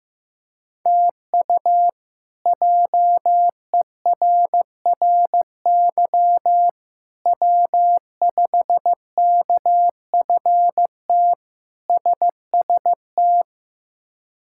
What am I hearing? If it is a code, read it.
Morse code, heard as a keyed tone, about 15 wpm, 700 Hz. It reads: TU JERRY W5KFT SST
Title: K1USN SST QSO Practice - 15wpm